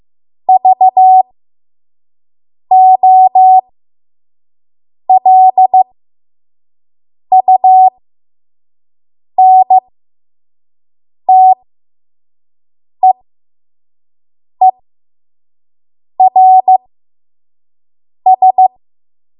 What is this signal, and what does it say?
Morse code, heard as a keyed tone, about 15 wpm, spaced as for 5 wpm, 750 Hz. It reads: VOLUNTEERS
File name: Morse / 05WPM